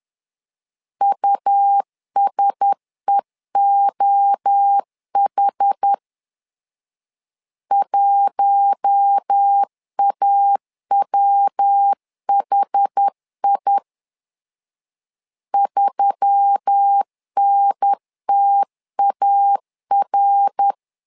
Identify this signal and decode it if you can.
Morse code, heard as a keyed tone, about 10 wpm, 800 Hz. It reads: USEOH 1AWHI 3NTAR